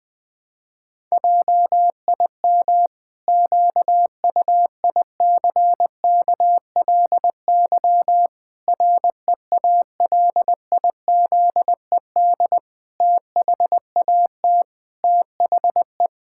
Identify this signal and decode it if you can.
Morse code, heard as a keyed tone, about 20 wpm, 700 Hz. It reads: JIM QUICKLY REALIZED THAT THE
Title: Pangram Challenge v2 - 20wpm